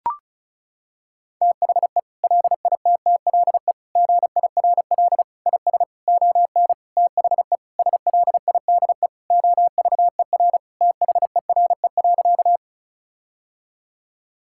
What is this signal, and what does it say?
Morse code, heard as a keyed tone, about 36 wpm, 700 Hz. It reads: THE LITTLE GIRL IS ON THE SLIDE OVER THERE.
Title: Sentences from Top 1600 Words 35wpm